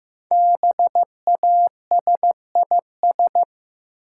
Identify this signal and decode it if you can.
Morse code, heard as a keyed tone, about 16 wpm, 700 Hz. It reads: BASIS
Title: Sentences from Top 1200 Words - New Word Drill Alt - 15wpm